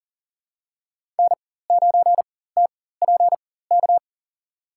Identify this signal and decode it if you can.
Morse code, heard as a keyed tone, about 40 wpm, spaced as for 19 wpm, 700 Hz. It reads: N9TPK
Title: USA Call Signs - 5 Characters - 4x Character Spacing - 40wpm